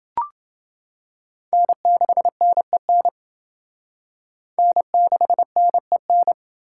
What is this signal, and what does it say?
Morse code, heard as a keyed tone, about 27 wpm, 700 Hz. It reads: N6NEN N6NEN